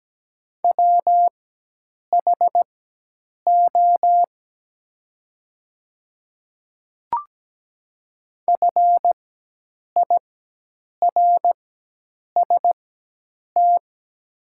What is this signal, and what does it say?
Morse code, heard as a keyed tone, about 17 wpm, spaced as for 8 wpm, 700 Hz. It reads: WHO FIRST
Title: 4x Character Spacing - Top 100 Words - 17wpm